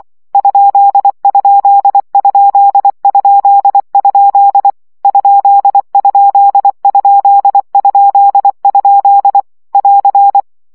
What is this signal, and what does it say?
Morse code, heard as a keyed tone, about 24 wpm, 800 Hz. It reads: ????? ????? <AR>
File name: CW gyakorlas kerdojel 20 baud